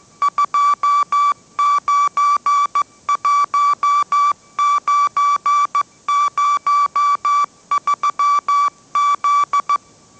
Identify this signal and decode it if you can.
Morse code, heard as a keyed tone, about 15 wpm, 1160 Hz. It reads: 291903Z
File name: mrx / mrx066 16wpm